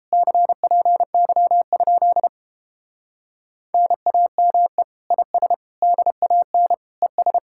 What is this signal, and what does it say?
Morse code, heard as a keyed tone, about 29 wpm, 700 Hz. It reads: CPY? NAME IS DAN ES